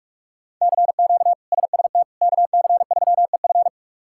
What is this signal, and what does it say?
Morse code, heard as a keyed tone, about 46 wpm, 700 Hz. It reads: CQ SST KC3EF
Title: K1USN SST QSO Practice - 45wpm